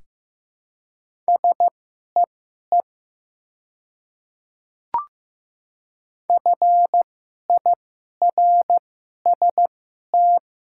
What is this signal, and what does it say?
Morse code, heard as a keyed tone, about 15 wpm, spaced as for 11 wpm, 700 Hz. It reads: SEE FIRST
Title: 2x Character Spacing - Top 500 Words - 15wpm